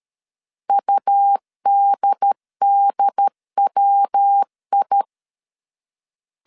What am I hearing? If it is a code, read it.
Morse code, heard as a keyed tone, about 13 wpm, 800 Hz. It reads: UDDWI
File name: CW / DET / 1 3 1500C S B 12wpm 2828